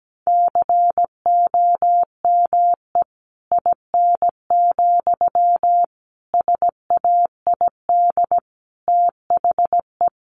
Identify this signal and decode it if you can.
Morse code, heard as a keyed tone, about 17 wpm, 700 Hz. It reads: COME IN, SAID THE
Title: Sentences from Top 1500 Words 17wpm